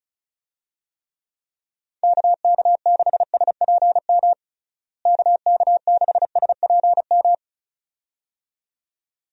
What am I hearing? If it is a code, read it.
Morse code, heard as a keyed tone, about 34 wpm, 700 Hz. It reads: KK6SPM KK6SPM
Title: USA Call Signs - 6 Characters - Repeated 2x - 35wpm